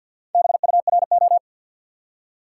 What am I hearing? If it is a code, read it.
Morse code, heard as a keyed tone, about 48 wpm, 700 Hz. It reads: BURO
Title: Top 90 QSO Elements 50wpm